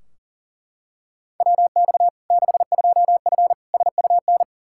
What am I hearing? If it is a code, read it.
Morse code, heard as a keyed tone, about 40 wpm, 700 Hz. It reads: WX 62F SUN
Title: Practice QSO - Exchange 2 - Info - 40wpm